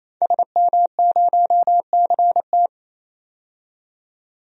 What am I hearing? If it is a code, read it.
Morse code, heard as a keyed tone, about 28 wpm, 700 Hz. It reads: SM0CT
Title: DXCC-100 Call Signs - 5 Characters - 28wpm